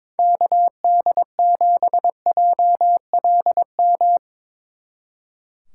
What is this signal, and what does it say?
Morse code, heard as a keyed tone, about 24 wpm, 700 Hz. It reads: KD7JLM
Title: Speed Racing - USA Call Signs 22wpm